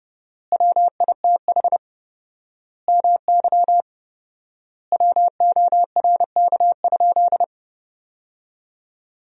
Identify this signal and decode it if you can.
Morse code, heard as a keyed tone, about 31 wpm, 700 Hz. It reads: WITH MY WORK?
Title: 4x Word-Spacing - Sentences from Top 500 Words - 30wpm